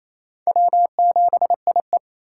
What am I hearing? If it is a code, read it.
Morse code, heard as a keyed tone, about 33 wpm, 700 Hz. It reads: W7IE
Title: Speed Racing - USA Call Signs 28wpm